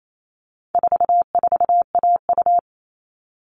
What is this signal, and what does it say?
Morse code, heard as a keyed tone, about 28 wpm, 700 Hz. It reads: 44AU
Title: All Cut Numbers - 4-Digit 28wpm